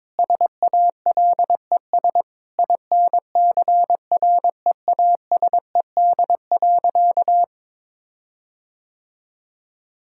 Morse code, heard as a keyed tone, about 22 wpm, 700 Hz. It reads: SALES INCREASED.
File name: Sentences with 7-Word Phrases - 22wpm